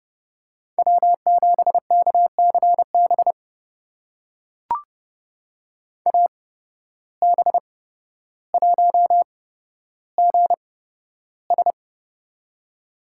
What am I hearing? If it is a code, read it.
Morse code, heard as a keyed tone, about 30 wpm, spaced as for 13 wpm, 700 Hz. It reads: W7KCB AB1GS
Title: USA Call Signs - 8x Character Spacing - 30wpm